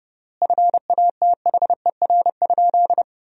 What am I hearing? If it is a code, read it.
Morse code, heard as a keyed tone, about 30 wpm, 700 Hz. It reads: FATHER?
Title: 3x Word-Spacing - Sentences from Top 400 Words - 30wpm